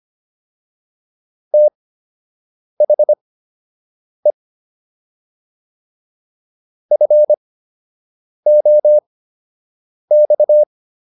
Morse code, heard as a keyed tone, about 25 wpm, spaced as for 7 wpm, 600 Hz. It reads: THE FOX